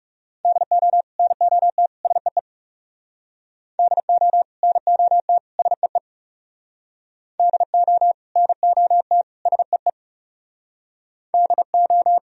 Morse code, heard as a keyed tone, about 37 wpm, 700 Hz. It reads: DO NOT SEE DO NOT SEE DO NOT SEE DO
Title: Speed-Racing - Sets of 3 Words 30wpm